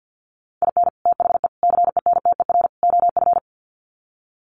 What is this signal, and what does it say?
Morse code, heard as a keyed tone, about 50 wpm, 700 Hz. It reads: IN THE CENTER OF
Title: Sets of 4 Words 50wpm